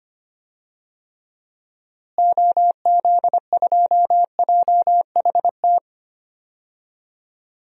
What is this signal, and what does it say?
Morse code, heard as a keyed tone, about 25 wpm, 700 Hz. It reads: OZ2JHT
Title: DXCC-100 Call Signs - 6 Characters - Repeated 2x - 25wpm